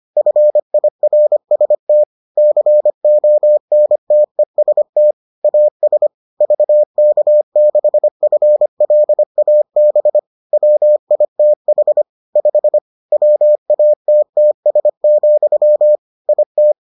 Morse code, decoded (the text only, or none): FIRST CONTEST AS VK6FLAB WITH 5 WATTS, IT